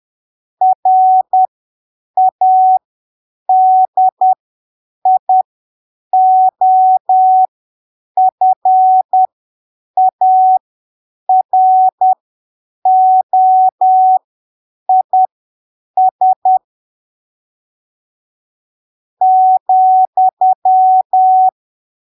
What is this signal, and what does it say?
Morse code, heard as a keyed tone, about 10 wpm, spaced as for 7 wpm, 750 Hz. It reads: RADIOFAROIS ,
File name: arquivos / texto / texto 03 10ppm